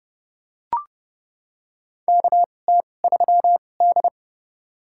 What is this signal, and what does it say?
Morse code, heard as a keyed tone, about 29 wpm, spaced as for 22 wpm, 700 Hz. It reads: KT3D
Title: USA Call Signs - 4 Characters - 2x Character Spacing - 30wpm